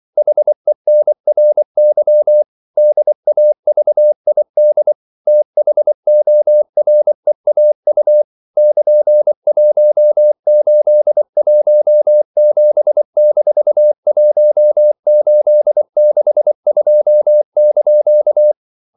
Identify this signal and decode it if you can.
Morse code, heard as a keyed tone, about 24 wpm, 600 Hz. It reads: HENRY DAVID THOREAU <KN>1817-1862)
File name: CW / Practice / EnglishQuotations0224